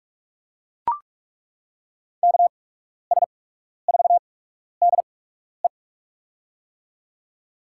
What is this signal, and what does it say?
Morse code, heard as a keyed tone, about 41 wpm, spaced as for 12 wpm, 700 Hz. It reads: KS4DE
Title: USA Call Signs - 5 Characters - 8x Character Spacing - 45wpm